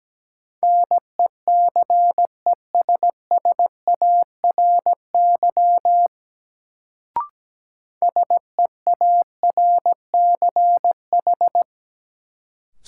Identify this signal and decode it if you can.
Morse code, heard as a keyed tone, about 17 wpm, 700 Hz. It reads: NECESSARY SEARCH
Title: Sentences from Top 800 Words - New Word Drill - 17wpm